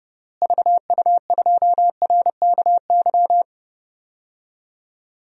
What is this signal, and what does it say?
Morse code, heard as a keyed tone, about 32 wpm, 700 Hz. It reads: VU2RKY
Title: DXCC-100 Call Signs - 6 Characters - Repeated 3x - 30wpm